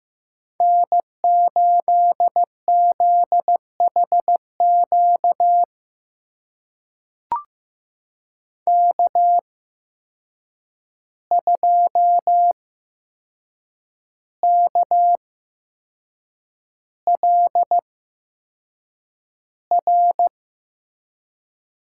Morse code, heard as a keyed tone, about 15 wpm, spaced as for 6 wpm, 700 Hz. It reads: N8ZHQ K2KLR
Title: USA Call Signs - 5 Characters - 8x Character Spacing - 15wpm